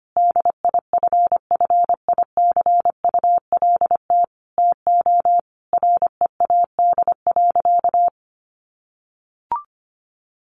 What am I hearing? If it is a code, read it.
Morse code, heard as a keyed tone, about 25 wpm, 700 Hz. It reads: DIFFICULT TO READ.
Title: Sentences from Top 1400 Words 25wpm